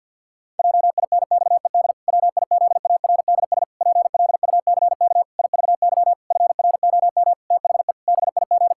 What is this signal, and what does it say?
Morse code, heard as a keyed tone, about 50 wpm, 700 Hz. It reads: JINXED WIZARDS PLUCK IVY FROM THE BIG
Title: Pangram Challenge v2 - 50wpm